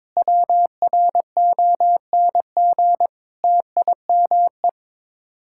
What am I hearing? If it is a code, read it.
Morse code, heard as a keyed tone, about 23 wpm, 700 Hz. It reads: WRONG TIME
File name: practice / 22wpm / Sets of 7 Words - 22wpm